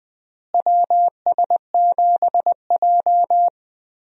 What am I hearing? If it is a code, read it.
Morse code, heard as a keyed tone, about 20 wpm, 700 Hz. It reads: WS7J
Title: USA Call Signs - 4 Characters - Repeated 3x - 20wpm